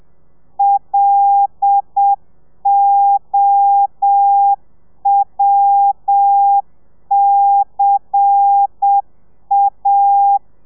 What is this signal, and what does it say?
Morse code, heard as a keyed tone, about 7 wpm, 800 Hz. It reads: LOWCA
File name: telegrafia / i2tfj / Lez10 30